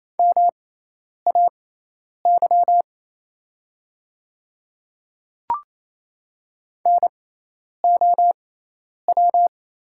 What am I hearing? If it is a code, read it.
Morse code, heard as a keyed tone, about 27 wpm, spaced as for 10 wpm, 700 Hz. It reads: MAY NOW
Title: 6x Character Spacing - Top 200 Words - 28wpm